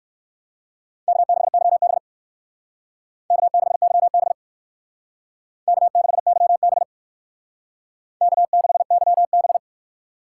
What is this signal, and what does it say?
Morse code, heard as a keyed tone, about 56 wpm, 700 Hz. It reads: K6YB K6YB K6YB K6YB
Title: Speed Racing - USA 1x2 Call Signs 45wpm